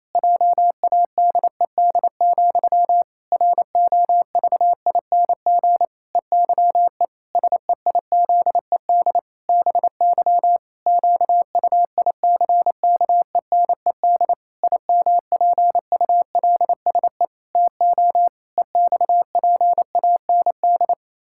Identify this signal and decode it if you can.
Morse code, heard as a keyed tone, about 28 wpm, 700 Hz. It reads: JADED, ROVING EYE SEIZED BY QUICKENED IMPULSE TO EXPAND
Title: Pangram Challenge v2 - 28wpm